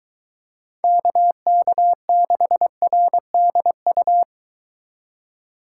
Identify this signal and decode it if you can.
Morse code, heard as a keyed tone, about 22 wpm, 700 Hz. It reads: KK6RDU